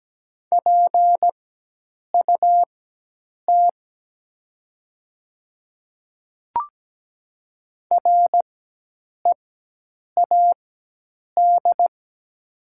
Morse code, heard as a keyed tone, about 17 wpm, spaced as for 8 wpm, 690 Hz. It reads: PUT READ
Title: 4x Character Spacing - Top 200 Words - 17wpm